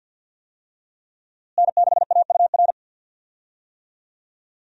Morse code, heard as a keyed tone, about 52 wpm, 700 Hz. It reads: N6AUR
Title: USA Call Signs - 5 Characters - Repeated 3x - 50wpm